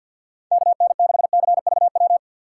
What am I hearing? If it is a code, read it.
Morse code, heard as a keyed tone, about 46 wpm, 700 Hz. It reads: KN6CVW
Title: USA Call Signs - 6 Characters - 4x Character Spacing - 50wpm